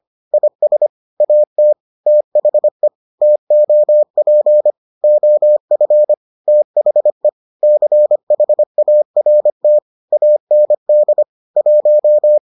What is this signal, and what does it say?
Morse code, heard as a keyed tone, about 25 wpm, 600 Hz. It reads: IS AT THE TOP OF THE CHART AND 1